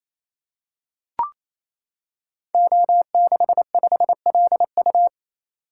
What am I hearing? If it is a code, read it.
Morse code, heard as a keyed tone, about 27 wpm, 700 Hz. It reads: O65LU
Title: Code Groups - 5 Letter:Numbers - 28wpm